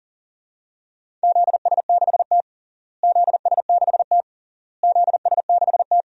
Morse code, heard as a keyed tone, about 40 wpm, 700 Hz. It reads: ZS6T ZS6T ZS6T
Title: DXCC-100 Call Signs - 4 Characters - Repeated 3x - 40wpm